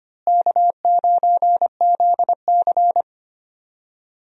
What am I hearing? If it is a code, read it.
Morse code, heard as a keyed tone, about 25 wpm, 700 Hz. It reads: K9ZC
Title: USA Call Signs - 4 Characters - Repeated 3x - 25wpm